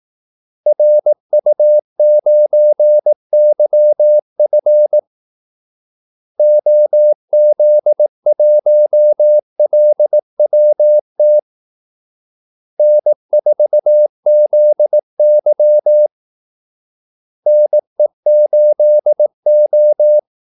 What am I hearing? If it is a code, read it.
Morse code, heard as a keyed tone, about 18 wpm, 600 Hz. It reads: RU9YF OZ1LWT N4ZY NE8O
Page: Kaldesignaler 18 wpm